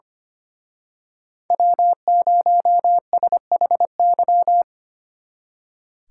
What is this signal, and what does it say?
Morse code, heard as a keyed tone, about 29 wpm, 700 Hz. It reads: W0SHY
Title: Speed Racing - USA 1x3 Call Signs 25wpm